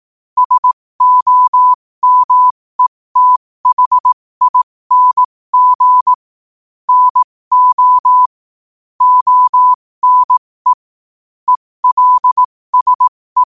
Morse code, heard as a keyed tone, about 18 wpm, spaced as for 15 wpm, 1000 Hz. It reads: SOMETHING NO ONE ELSE
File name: qotdfiles / 15wpm